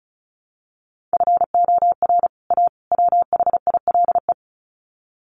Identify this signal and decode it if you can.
Morse code, heard as a keyed tone, about 34 wpm, 700 Hz. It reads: FOR A WHILE
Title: Sets of 3 Words 35wpm